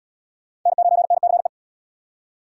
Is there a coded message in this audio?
Yes: Morse code from 0.7 to 1.5 s, heard as a keyed tone, about 69 wpm, 700 Hz. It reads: N9NCE